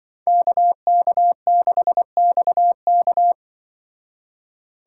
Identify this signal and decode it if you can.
Morse code, heard as a keyed tone, about 24 wpm, 700 Hz. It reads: KK6XK
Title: Speed Racing - USA 2x2 Call Signs 20wpm